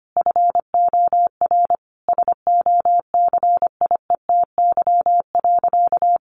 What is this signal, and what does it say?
Morse code, heard as a keyed tone, about 25 wpm, 700 Hz. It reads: FOR SOCIETY.
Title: Sentences from Top 700 Words 25wpm